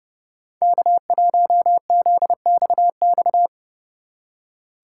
Morse code, heard as a keyed tone, about 30 wpm, 700 Hz. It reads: K1ZXX
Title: USA Call Signs - 5 Characters - 3x Character Spacing - 30wpm